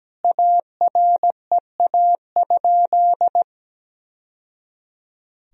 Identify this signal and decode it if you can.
Morse code, heard as a keyed tone, about 17 wpm, 700 Hz. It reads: AREA?
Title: Sentences from Top 1000 Words - Review Part 3 - 17wpm